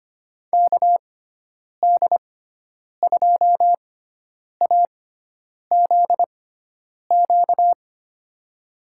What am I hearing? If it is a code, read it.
Morse code, heard as a keyed tone, about 25 wpm, spaced as for 9 wpm, 700 Hz. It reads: KD2AZQ
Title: USA Call Signs - 6x Character Spacing - 25wpm